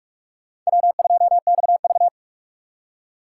Morse code, heard as a keyed tone, about 44 wpm, 690 Hz. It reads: W2XV
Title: USA Call Signs - 4 Characters - 4x Character Spacing - 45wpm